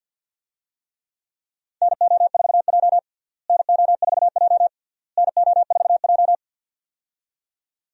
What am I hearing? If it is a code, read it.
Morse code, heard as a keyed tone, about 46 wpm, 700 Hz. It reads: NO4J NO4J NO4J
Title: USA Call Signs - 4 Characters - Repeated 3x - 50wpm